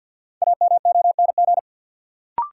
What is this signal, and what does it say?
Morse code, heard as a keyed tone, about 46 wpm, 700 Hz. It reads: AMONG
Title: Sentences from Top 500 Words - New Word Drill - 50wpm